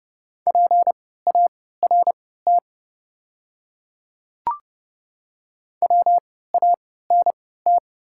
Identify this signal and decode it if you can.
Morse code, heard as a keyed tone, about 28 wpm, spaced as for 17 wpm, 690 Hz. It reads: PART WANT
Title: 3x Character Spacing - Top 300 Words - 30wpm